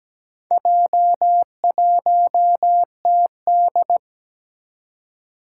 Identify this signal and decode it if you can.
Morse code, heard as a keyed tone, about 17 wpm, 700 Hz. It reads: J1TD
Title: Four Challenging Letter-Number Combinations 17wpm